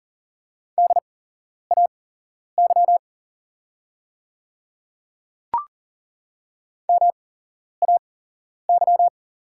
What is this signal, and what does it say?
Morse code, heard as a keyed tone, about 40 wpm, spaced as for 11 wpm, 690 Hz. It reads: DAY MAY
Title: 8x Character Spacing - Top 100 Words - 40wpm